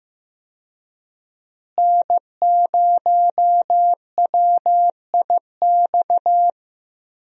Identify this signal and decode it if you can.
Morse code, heard as a keyed tone, about 15 wpm, 700 Hz. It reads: N0WIX
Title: USA Call Signs - 5 Characters - Repeated 3x - 15wpm